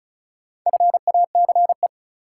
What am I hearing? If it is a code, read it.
Morse code, heard as a keyed tone, about 35 wpm, 700 Hz. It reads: FACE
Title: Sentences from Top 300 Words - New Word Drill - 35wpm